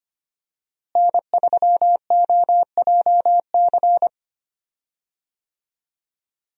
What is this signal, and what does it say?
Morse code, heard as a keyed tone, about 25 wpm, 700 Hz. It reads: N3OJC
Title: USA Call Signs - 5 Characters - Repeated 2x - 25wpm